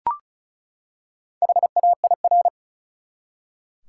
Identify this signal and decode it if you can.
Morse code, heard as a keyed tone, about 34 wpm, 700 Hz. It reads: HAIR
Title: Four Letter Words - 35wpm